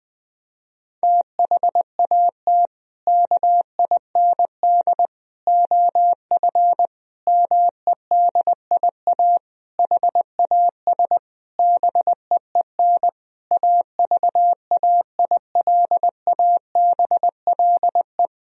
Morse code, heard as a keyed tone, about 20 wpm, 700 Hz. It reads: THAT KIND OF MEDIA HAS BEEN AVAILABLE
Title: Sentences from Top 1000 Words - Review Part 4 - 20wpm